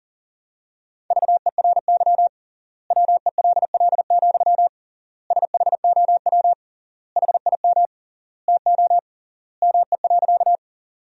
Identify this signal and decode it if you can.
Morse code, heard as a keyed tone, about 37 wpm, 700 Hz. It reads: VERY WELL, SHOW HIM TO ME.
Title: 3x Word-Spacing - Sentences from Top 200 Words - 40wpm